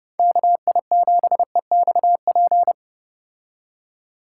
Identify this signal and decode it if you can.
Morse code, heard as a keyed tone, about 29 wpm, 700 Hz. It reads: KI7EXP